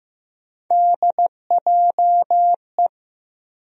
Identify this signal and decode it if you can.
Morse code, heard as a keyed tone, about 15 wpm, 700 Hz. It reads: DJE
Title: USA Call Sign Postfixes - 3 Letters 15wpm